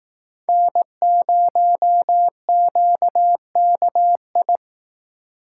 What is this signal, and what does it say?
Morse code, heard as a keyed tone, about 18 wpm, 700 Hz. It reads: N0QKI
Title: Speed Racing - USA 1x3 Call Signs 15wpm